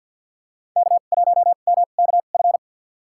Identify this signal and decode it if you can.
Morse code, heard as a keyed tone, about 51 wpm, spaced as for 36 wpm, 700 Hz. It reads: K1MKF
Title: USA Call Signs - 5 Characters - 2x Character Spacing - 50wpm